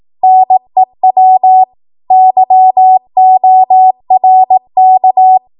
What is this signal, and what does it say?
Morse code, heard as a keyed tone, about 18 wpm, 750 Hz. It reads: NEW YORK
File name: Archive / 18 WPM / 18WPM